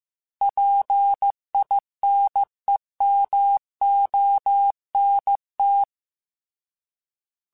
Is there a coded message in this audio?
Yes: Morse code from 0.4 to 5.8 s, heard as a keyed tone, about 15 wpm, 800 Hz. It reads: PINEMONT